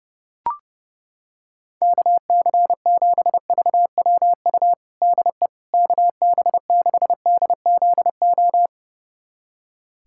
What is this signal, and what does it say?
Morse code, heard as a keyed tone, about 32 wpm, 700 Hz. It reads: KC7VWU DE KB6DZO
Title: USA Callsign DE Callsign 30wpm